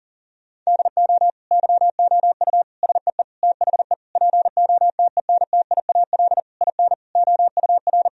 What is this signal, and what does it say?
Morse code, heard as a keyed tone, about 40 wpm, 700 Hz. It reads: DO YOU SEE THE POTENTIAL IN OUR
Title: Sentences from Top 1000 Words - Review Part 4 - 40wpm